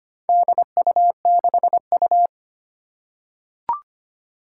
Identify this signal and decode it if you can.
Morse code, heard as a keyed tone, about 25 wpm, 700 Hz. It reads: DU6U
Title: Four Challenging Letter-Number Combinations 25wpm